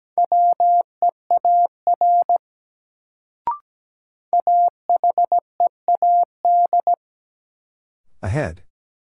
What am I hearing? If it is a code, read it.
Morse code, heard as a keyed tone, about 17 wpm, 700 Hz. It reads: WEAR AHEAD
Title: Sentences from Top 1000 Words - Review Part 6 - New Word Drill - 17wpm